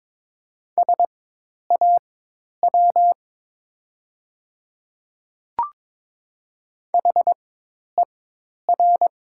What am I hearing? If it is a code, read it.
Morse code, heard as a keyed tone, about 22 wpm, spaced as for 10 wpm, 700 Hz. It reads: SAW HER
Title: 4x Character Spacing - Top 500 Words - 22wpm